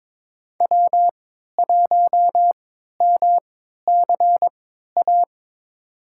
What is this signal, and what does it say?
Morse code, heard as a keyed tone, about 22 wpm, spaced as for 12 wpm, 700 Hz. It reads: W1MCA
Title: USA Call Signs - 5 Characters - 3x Character Spacing - 22wpm